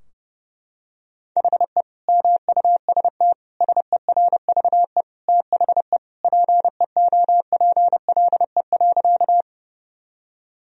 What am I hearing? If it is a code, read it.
Morse code, heard as a keyed tone, about 32 wpm, 700 Hz. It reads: HE MUST SERVE THE PEOPLE.
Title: Sentences from Top 1000 Words - Review Part 2 - 30wpm